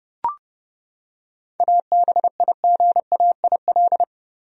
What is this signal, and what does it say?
Morse code, heard as a keyed tone, about 31 wpm, 700 Hz. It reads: ABIGAIL
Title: USA Common Names - 30wpm